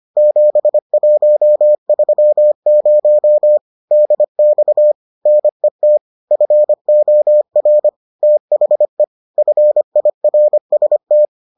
Morse code, heard as a keyed tone, about 25 wpm, 600 Hz. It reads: 7130 DX NET FOR THE FIRST